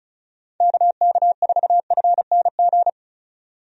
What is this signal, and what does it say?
Morse code, heard as a keyed tone, about 36 wpm, 700 Hz. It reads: KK4FNG
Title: USA Call Signs - 6 Characters - 35wpm